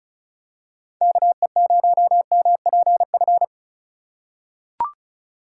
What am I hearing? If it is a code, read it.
Morse code, heard as a keyed tone, about 36 wpm, 700 Hz. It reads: KE0MPF
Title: USA Call Signs - 6 Characters - 3x Character Spacing - 35wpm